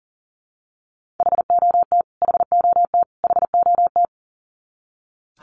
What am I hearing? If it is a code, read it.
Morse code, heard as a keyed tone, about 40 wpm, 700 Hz. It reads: HOT HOT HOT
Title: Top 300 Words Repeated 40wpm